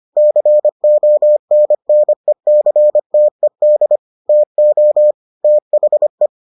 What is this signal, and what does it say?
Morse code, heard as a keyed tone, about 25 wpm, 600 Hz. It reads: CONNECTED TO THE